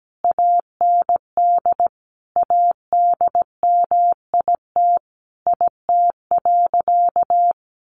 Morse code, heard as a keyed tone, about 17 wpm, 700 Hz. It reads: AND ADMIT IT.
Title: A Sound Of Thunder 17wpm